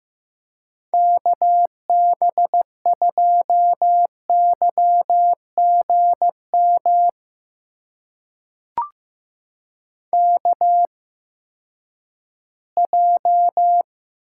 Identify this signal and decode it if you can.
Morse code, heard as a keyed tone, about 15 wpm, spaced as for 10 wpm, 700 Hz. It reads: KB2YGM KJ
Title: USA Call Signs - 6 Characters - 8x Character Spacing - 15wpm